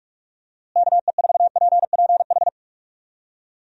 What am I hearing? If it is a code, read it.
Morse code, heard as a keyed tone, about 46 wpm, 700 Hz. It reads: KE4PPH
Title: USA Call Signs - 6 Characters - Repeated 3x - 45wpm